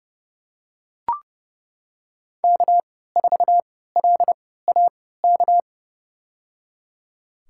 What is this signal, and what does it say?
Morse code, heard as a keyed tone, about 30 wpm, spaced as for 17 wpm, 700 Hz. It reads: K4LAK
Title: USA Call Signs - 5 Characters - 3x Character Spacing - 30wpm